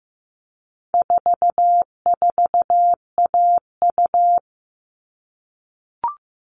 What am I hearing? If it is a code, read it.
Morse code, heard as a keyed tone, about 15 wpm, 700 Hz. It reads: 44AU
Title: All Cut Numbers - 4-Digit 15wpm